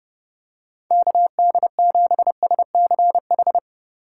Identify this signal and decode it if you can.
Morse code, heard as a keyed tone, about 31 wpm, 700 Hz. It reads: KD7SCH